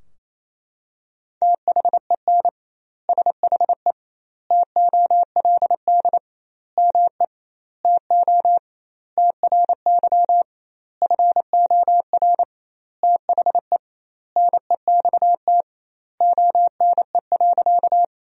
Morse code, heard as a keyed tone, about 28 wpm, 700 Hz. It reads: THEN SHE TOLD ME TO TRY FOR THE NEXT ONE.
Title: 2x Word-Spacing - Sentences from Top 400 Words - 28wpm